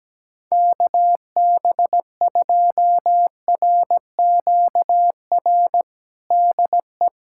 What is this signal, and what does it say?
Morse code, heard as a keyed tone, about 17 wpm, 700 Hz. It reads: KB2RQR DE
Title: USA Callsign DE Callsign - 6 Characters - 17wpm